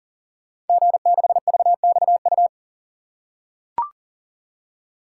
Morse code, heard as a keyed tone, about 42 wpm, 690 Hz. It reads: G6VXU
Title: Code Groups - 5 Letter:Numbers - 40wpm